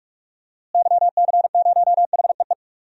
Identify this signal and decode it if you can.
Morse code, heard as a keyed tone, about 43 wpm, 700 Hz. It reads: YC0HEE
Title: DXCC-100 Call Signs - 6 Characters - Repeated 2x - 45wpm